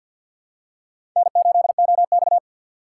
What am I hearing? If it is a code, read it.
Morse code, heard as a keyed tone, about 53 wpm, 700 Hz. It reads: N8OX
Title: USA Call Signs - 6x Character Spacing - 50wpm